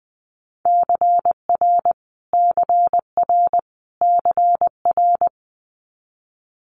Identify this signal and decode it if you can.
Morse code, heard as a keyed tone, about 20 wpm, 700 Hz. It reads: CR CR CR
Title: Multi-letter Phonemes Repeated 3x 20wpm